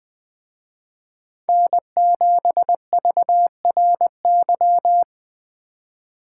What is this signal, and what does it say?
Morse code, heard as a keyed tone, about 20 wpm, 700 Hz. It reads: N7VRY